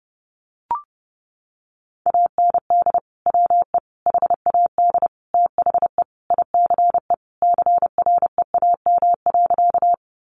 Morse code, heard as a keyed tone, about 30 wpm, 700 Hz. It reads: AND WE HAD THE ICE CREAM.
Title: Sentences from Top 1500 Words 30wpm